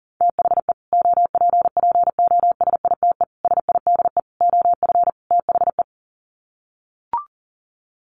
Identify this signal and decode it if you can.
Morse code, heard as a keyed tone, about 37 wpm, 700 Hz. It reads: THE OPPOSITE SIDE OF THE
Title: Sets of 5 Words Encore 40wpm